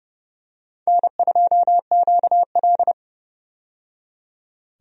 Morse code, heard as a keyed tone, about 29 wpm, 700 Hz. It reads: N2QL
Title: USA Call Signs - 4 Characters - 30wpm